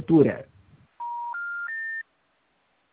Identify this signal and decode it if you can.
Special information tone, with speech.
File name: out- -2003- - - .46032